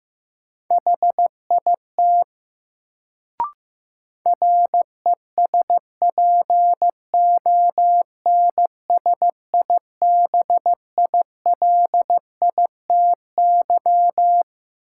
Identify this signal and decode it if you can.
Morse code, heard as a keyed tone, about 15 wpm, 700 Hz. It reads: HIT RESPONSIBILITY
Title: Sentences from Top 1200 Words - New Word Drill Alt - 15wpm